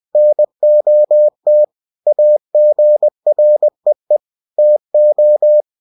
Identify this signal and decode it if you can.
Morse code, heard as a keyed tone, about 20 wpm, 600 Hz. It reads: NOT AGREE TO
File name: cw-books / winnie-the-pooh / winnie-20wpm-page-0117